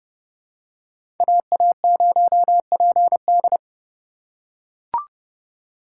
Morse code, heard as a keyed tone, about 32 wpm, 700 Hz. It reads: AA0PD
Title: USA Call Signs - 5 Characters - 6x Character Spacing - 30wpm